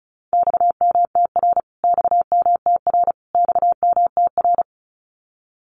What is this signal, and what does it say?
Morse code, heard as a keyed tone, about 34 wpm, 700 Hz. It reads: XMTR XMTR XMTR
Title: Abbreviations Repeated 3x - 35wpm